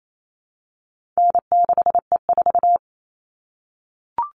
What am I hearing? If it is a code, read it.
Morse code, heard as a keyed tone, about 28 wpm, 700 Hz. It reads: N6E4
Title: All Cut Numbers - 4-Digit 28wpm